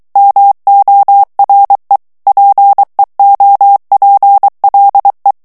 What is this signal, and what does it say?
Morse code, heard as a keyed tone, about 23 wpm, 800 Hz. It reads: MORE PEOPLE
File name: morse mp3 / ANGLAIS texte clair 2 vitesse 22 mots minute